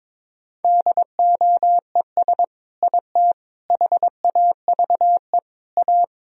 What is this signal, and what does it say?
Morse code, heard as a keyed tone, about 22 wpm, 700 Hz. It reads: DOES IT HAVE A
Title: Sentences from Top 1000 Words - Review Part 3 - 22wpm